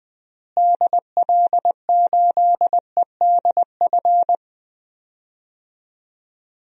Morse code, heard as a keyed tone, about 20 wpm, 700 Hz. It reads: DL8EDF
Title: DXCC-100 Call Signs - 6 Characters - Repeated 3x - 20wpm